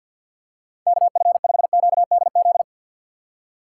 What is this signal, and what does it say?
Morse code, heard as a keyed tone, about 53 wpm, 700 Hz. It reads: KF5QDZ
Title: USA Call Signs - 6 Characters - 6x Character Spacing - 50wpm